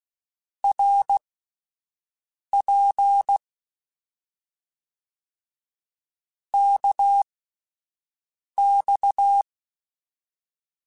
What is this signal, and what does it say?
Morse code, heard as a keyed tone, about 16 wpm, spaced as for 5 wpm, 800 Hz. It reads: RP KX
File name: Short Runs / Short Runs-mp3s / Balanced @10